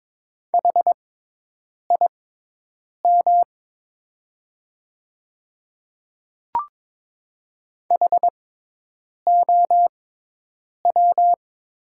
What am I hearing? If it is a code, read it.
Morse code, heard as a keyed tone, about 22 wpm, spaced as for 8 wpm, 700 Hz. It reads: HIM HOW
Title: 6x Character Spacing - Top 200 Words - 22wpm